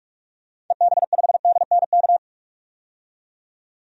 Morse code, heard as a keyed tone, about 41 wpm, 700 Hz. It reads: EB5DNK
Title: DXCC-100 Call Signs - 6 Characters - 45wpm